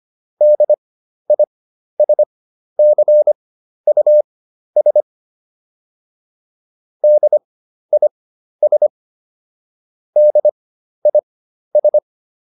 Morse code, heard as a keyed tone, about 25 wpm, spaced as for 12 wpm, 600 Hz. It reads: DISCUS DIS DIS